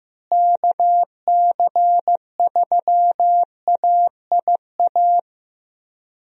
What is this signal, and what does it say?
Morse code, heard as a keyed tone, about 16 wpm, 700 Hz. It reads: KC3AIA
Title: USA Call Signs - 6 Characters - Repeated 3x - 15wpm